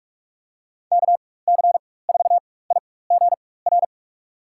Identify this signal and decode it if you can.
Morse code, heard as a keyed tone, about 47 wpm, spaced as for 21 wpm, 700 Hz. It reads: KC4IGR
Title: USA Call Signs - 6 Characters - 4x Character Spacing - 45wpm